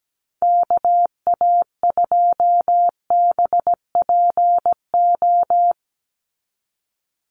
Morse code, heard as a keyed tone, about 17 wpm, 700 Hz. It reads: KA2BPO
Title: USA Call Signs 17wpm